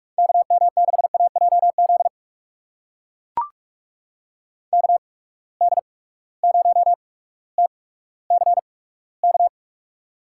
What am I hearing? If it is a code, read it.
Morse code, heard as a keyed tone, about 42 wpm, spaced as for 19 wpm, 700 Hz. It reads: KM6AJZ KD0TCK
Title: USA Call Signs - 6 Characters - 8x Character Spacing - 45wpm